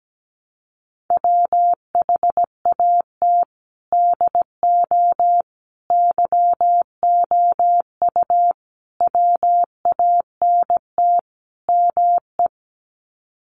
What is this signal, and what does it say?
Morse code, heard as a keyed tone, about 17 wpm, 700 Hz. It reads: WHAT DO YOU WANT ME
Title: Sets of 5 Words Encore2 17wpm